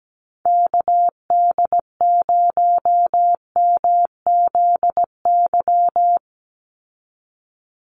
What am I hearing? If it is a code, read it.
Morse code, heard as a keyed tone, about 17 wpm, 700 Hz. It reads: KD0MZY
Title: USA Call Signs 17wpm